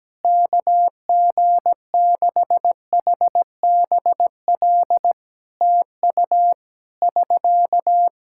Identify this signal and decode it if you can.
Morse code, heard as a keyed tone, about 17 wpm, 700 Hz. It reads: KG6HBL TU <SK>
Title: Practice QSO - Exchange 4 - Ending - 17wpm